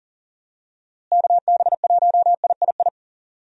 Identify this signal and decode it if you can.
Morse code, heard as a keyed tone, about 40 wpm, 700 Hz. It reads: KB1III
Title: USA Call Signs - 6 Characters - Repeated 3x - 40wpm